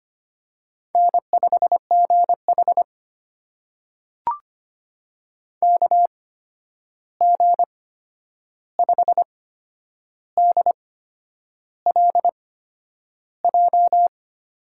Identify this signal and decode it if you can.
Morse code, heard as a keyed tone, about 25 wpm, spaced as for 9 wpm, 700 Hz. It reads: N5GH KG5DLJ
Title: USA Call Signs - 8x Character Spacing - 25wpm